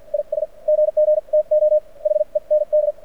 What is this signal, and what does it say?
Morse code, heard as a keyed tone, about 48 wpm, spaced as for 41 wpm, 600 Hz. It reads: IS GG TO HEND
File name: archived / CRSnet